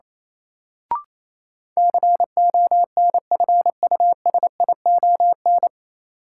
Morse code, heard as a keyed tone, about 28 wpm, 700 Hz. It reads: CONFUSION
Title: Sentences from Top 1600 Words - New Word Drill - 28wpm